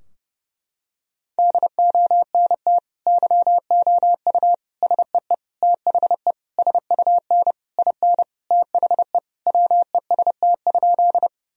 Morse code, heard as a keyed tone, about 29 wpm, 700 Hz. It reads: DONT YOU SEE THE SUN IN THE WEST?
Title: Sentences from Top 1000 Words - Review Part 2 - 30wpm